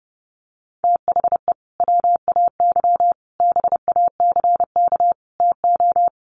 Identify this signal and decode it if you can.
Morse code, heard as a keyed tone, about 30 wpm, 700 Hz. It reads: THE WAY BACK TO
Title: Sets of 5 Words Encore2 30wpm